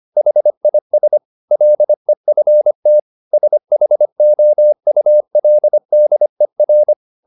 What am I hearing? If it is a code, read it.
Morse code, heard as a keyed tone, about 25 wpm, 600 Hz. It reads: HIS LEFT SHOULDER
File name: the old man and the sea part 1 25wpm